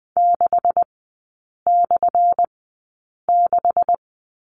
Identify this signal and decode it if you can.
Morse code, heard as a keyed tone, about 20 wpm, spaced as for 8 wpm, 700 Hz. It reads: 6/6
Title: 184 Lesson 30e - Letter b - Three Characters 20wpm